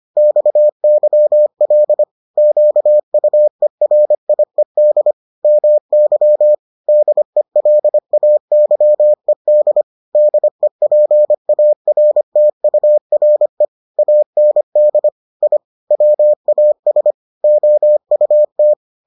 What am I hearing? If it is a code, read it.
Morse code, heard as a keyed tone, about 25 wpm, 600 Hz. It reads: XYL QUERIED MY DELAYED DEPARTURE AND I WAS OUT